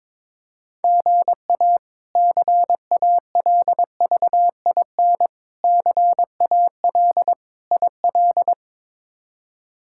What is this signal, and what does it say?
Morse code, heard as a keyed tone, about 22 wpm, 700 Hz. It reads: GA CALVIN CAL IL
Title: K1USN SST QSO Practice - 22wpm